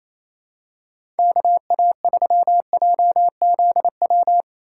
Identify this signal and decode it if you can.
Morse code, heard as a keyed tone, about 28 wpm, 700 Hz. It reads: KA3JZW
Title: USA Call Signs - 6 Characters - 2x Character Spacing - 28wpm